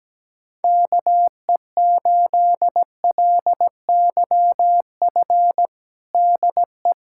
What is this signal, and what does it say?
Morse code, heard as a keyed tone, about 17 wpm, 700 Hz. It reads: KE8LYF DE
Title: USA Callsign DE Callsign 17wpm